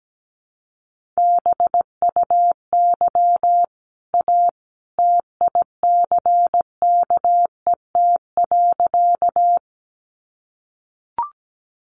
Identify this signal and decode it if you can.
Morse code, heard as a keyed tone, about 17 wpm, 700 Hz. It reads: BUY A TICKET.
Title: Sentences from Top 1400 Words 17wpm